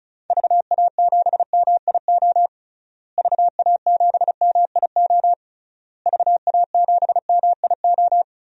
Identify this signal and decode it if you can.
Morse code, heard as a keyed tone, about 36 wpm, 700 Hz. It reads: VA7MIO VA7MIO VA7MIO
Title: DXCC-100 Call Signs - 6 Characters - Repeated 3x - 35wpm